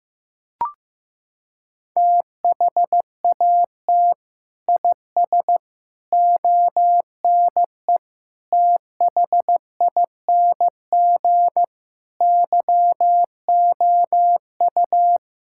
Morse code, heard as a keyed tone, about 15 wpm, 700 Hz. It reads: THAT IS ONE THING YOU
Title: Sentences from Top 1600 Words 15wpm